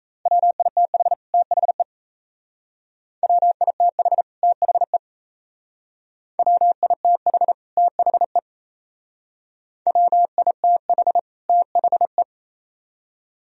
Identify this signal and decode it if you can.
Morse code, heard as a keyed tone, about 32 wpm, 700 Hz. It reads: WITH THE WITH THE WITH THE WITH THE
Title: Speed-Racing - Sets of 2 Words 28wpm